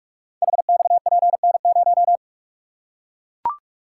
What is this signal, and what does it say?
Morse code, heard as a keyed tone, about 44 wpm, 700 Hz. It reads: HXPN0
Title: Code Groups - 5 Letter:Numbers - 45wpm